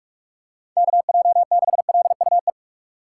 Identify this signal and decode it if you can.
Morse code, heard as a keyed tone, about 41 wpm, 700 Hz. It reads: KJ6LUE
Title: USA Call Signs - 4x Character Spacing - 45wpm